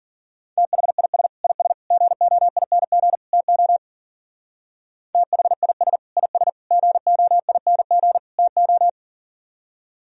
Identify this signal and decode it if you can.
Morse code, heard as a keyed tone, about 52 wpm, 700 Hz. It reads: THIS IS GOING TO THIS IS GOING TO